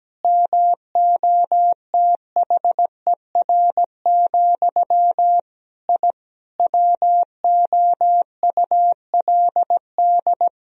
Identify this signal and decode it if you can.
Morse code, heard as a keyed tone, about 17 wpm, 700 Hz. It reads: MOTHER, I WOULD
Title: Sentences from Top 1000 Words - Review Part 2 - 17wpm